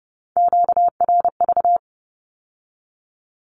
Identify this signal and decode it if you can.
Morse code, heard as a keyed tone, about 30 wpm, 700 Hz. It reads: QRV
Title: Q-Codes with Spoken Meaning - 30wpm